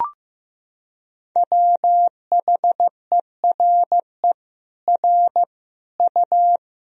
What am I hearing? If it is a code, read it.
Morse code, heard as a keyed tone, about 16 wpm, 700 Hz. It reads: WHERE R U
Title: Sentences from Top 200 Words and Abbreviations 15wpm